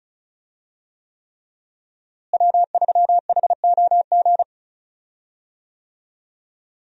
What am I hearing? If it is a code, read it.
Morse code, heard as a keyed tone, about 36 wpm, 700 Hz. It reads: W3HOG
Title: USA Call Signs - 5 Characters - Repeated 3x - 35wpm